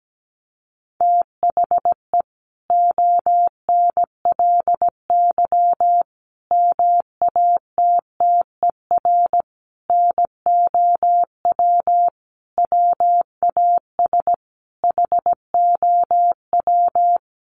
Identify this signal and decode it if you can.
Morse code, heard as a keyed tone, about 17 wpm, 700 Hz. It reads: THE ONLY MATTER NOW WAS HOW
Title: Sentences from Top 1400 Words 17wpm